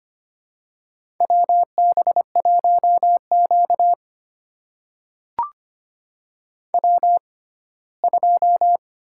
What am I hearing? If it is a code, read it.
Morse code, heard as a keyed tone, about 25 wpm, spaced as for 17 wpm, 700 Hz. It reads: WB1Q W2
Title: USA Call Signs - 4 Characters - 6x Character Spacing - 25wpm